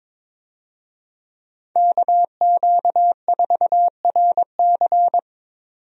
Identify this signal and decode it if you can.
Morse code, heard as a keyed tone, about 22 wpm, 700 Hz. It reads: KQ4RC
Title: USA Call Signs - 5 Characters - 22wpm